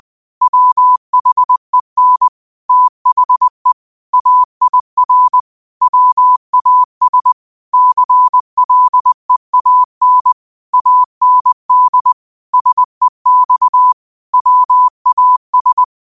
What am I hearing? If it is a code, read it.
Morse code, heard as a keyed tone, about 20 wpm, 1000 Hz. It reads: WHEN THE AIR WAS CLEAN AND SEX WAS
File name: qotdfiles / 20wpm